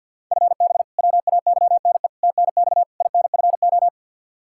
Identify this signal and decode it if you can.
Morse code, heard as a keyed tone, about 51 wpm, 700 Hz. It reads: FB WAYNE TNX INFO
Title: K1USN SST QSO Practice 50wpm